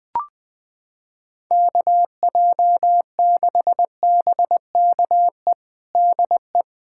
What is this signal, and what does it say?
Morse code, heard as a keyed tone, about 20 wpm, 700 Hz. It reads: KJ6BKE DE
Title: USA Callsign DE Callsign - 6 Characters - 20wpm